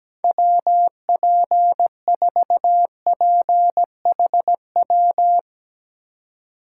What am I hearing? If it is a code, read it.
Morse code, heard as a keyed tone, about 17 wpm, 700 Hz. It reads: WP4PHW
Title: USA Call Signs - 4x Character Spacing - 17wpm